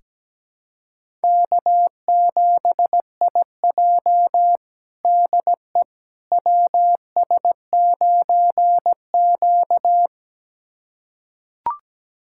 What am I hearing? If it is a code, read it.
Morse code, heard as a keyed tone, about 17 wpm, 700 Hz. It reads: K7IJ DE WS9Q
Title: USA Callsign DE Callsign - 4 Characters - 17wpm